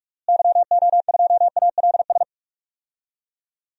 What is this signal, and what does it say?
Morse code, heard as a keyed tone, about 41 wpm, 700 Hz. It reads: YO2ALS
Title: DXCC-100 Call Signs - 6 Characters - Repeated 3x - 45wpm